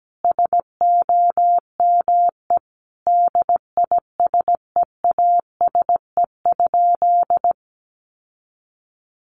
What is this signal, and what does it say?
Morse code, heard as a keyed tone, about 17 wpm, 700 Hz. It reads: SOME DISEASE?
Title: Sentences from Top 1500 Words 17wpm